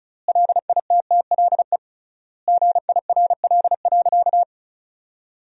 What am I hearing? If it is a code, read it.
Morse code, heard as a keyed tone, about 34 wpm, 700 Hz. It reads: LITTLE GIRL.
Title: 3x Word-Spacing - Sentences from Top 400 Words - 35wpm